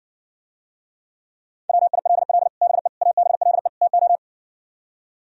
Morse code, heard as a keyed tone, about 60 wpm, 700 Hz. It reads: WILL BE ABLE TO